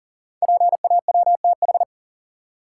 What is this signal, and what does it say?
Morse code, heard as a keyed tone, about 38 wpm, 690 Hz. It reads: PAWTH
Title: Code Groups - 5 Letters - 40wpm